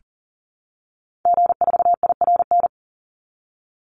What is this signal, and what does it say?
Morse code, heard as a keyed tone, about 37 wpm, 700 Hz. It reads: G4IRN
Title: Call Signs - 40wpm